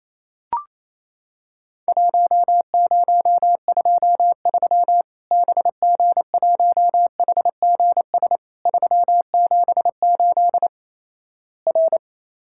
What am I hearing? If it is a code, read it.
Morse code, heard as a keyed tone, about 28 wpm, 700 Hz. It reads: 1023 BG1HGS 378 R
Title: Worked All Europe DX Contest - QTC Exchange - 28wpm